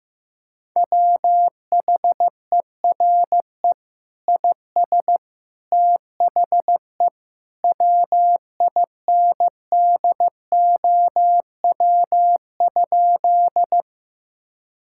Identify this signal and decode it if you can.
Morse code, heard as a keyed tone, about 16 wpm, 700 Hz. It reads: WHERE IS THE WINDOW?
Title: Sentences from Top 1600 Words 15wpm